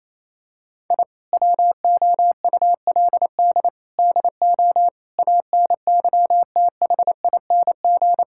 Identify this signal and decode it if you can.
Morse code, heard as a keyed tone, about 28 wpm, 700 Hz. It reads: I WOULD DO ANYTHING
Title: Sentences from Top 1000 Words - Review Part 6 - 28wpm